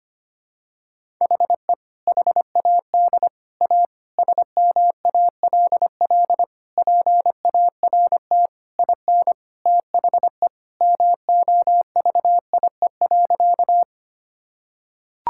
Morse code, heard as a keyed tone, about 25 wpm, 700 Hz. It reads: HE HAD A SMALL PART IN THE MOVIE.
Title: Sentences from Top 1000 Words - Review Part 5 - 25wpm